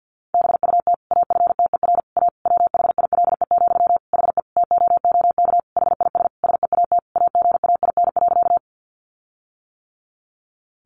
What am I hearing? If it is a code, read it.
Morse code, heard as a keyed tone, about 50 wpm, 700 Hz. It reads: BUT AFTER A WHILE, HE TOOK HIS SEAT AGAIN.
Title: Sentences from Top 1500 Words 50wpm